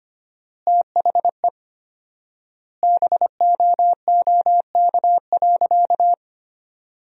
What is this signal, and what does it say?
Morse code, heard as a keyed tone, about 25 wpm, 700 Hz. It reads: THE BOOK.
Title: 4x Word-Spacing - Sentences from Top 300 Words - 25wpm